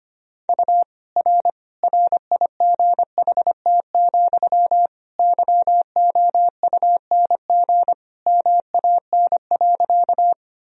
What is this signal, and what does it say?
Morse code, heard as a keyed tone, about 25 wpm, 700 Hz. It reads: U R RIGHT, YOUNG MAN.
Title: Sentences from Top 400 Words and Abbreviations 25wpm